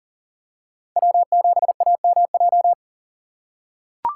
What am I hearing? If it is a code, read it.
Morse code, heard as a keyed tone, about 39 wpm, 700 Hz. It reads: W7AMJ
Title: USA Call Signs - 5 Characters - 3x Character Spacing - 40wpm